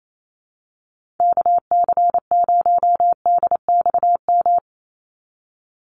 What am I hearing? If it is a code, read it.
Morse code, heard as a keyed tone, about 28 wpm, 700 Hz. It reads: KC0DXM
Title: USA Call Signs 28wpm